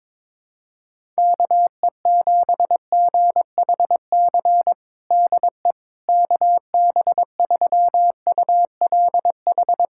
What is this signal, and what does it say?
Morse code, heard as a keyed tone, about 22 wpm, 700 Hz. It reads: KE7GHC DE KB3ULH
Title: USA Callsign DE Callsign - 6 Characters - 22wpm